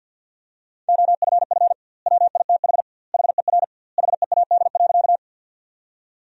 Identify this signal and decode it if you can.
Morse code, heard as a keyed tone, about 51 wpm, 700 Hz. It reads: OFF WITH HER HEAD.
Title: 2x Word-Spacing - Sentences from Top 300 Words - 50wpm